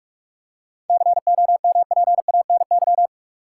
Morse code, heard as a keyed tone, about 42 wpm, 700 Hz. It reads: COMPANY
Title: Seven Letter Words - 45wpm